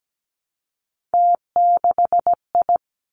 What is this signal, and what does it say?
Morse code, heard as a keyed tone, about 17 wpm, 700 Hz. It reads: T6I
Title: Three Challenging Letter-Number Combinations 17wpm